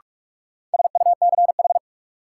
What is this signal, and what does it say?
Morse code, heard as a keyed tone, about 48 wpm, 700 Hz. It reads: SUCH
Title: Sets of 2 Words - New Word Drill - 45wpm